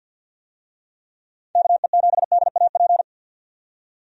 Morse code, heard as a keyed tone, about 48 wpm, 700 Hz. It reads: KE7DAP
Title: USA Call Signs - 6 Characters - 50wpm